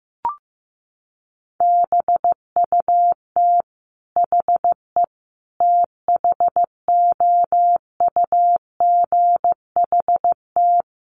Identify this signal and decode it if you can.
Morse code, heard as a keyed tone, about 15 wpm, 700 Hz. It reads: BUT HE THOUGHT
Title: Sentences from Top 1400 Words 15wpm